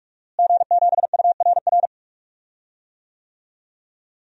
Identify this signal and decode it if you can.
Morse code, heard as a keyed tone, about 46 wpm, 700 Hz. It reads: G7URR
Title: DXCC-100 Call Signs - 5 Characters - Repeated 3x - 45wpm